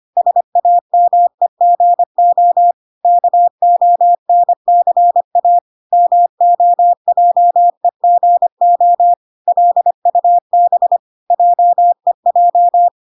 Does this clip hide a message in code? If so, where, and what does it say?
Yes: Morse code from 0.2 to 12.9 s, heard as a keyed tone, about 25 wpm, 700 Hz. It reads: SAMEGO KONCA MOJEGO LUB JEJ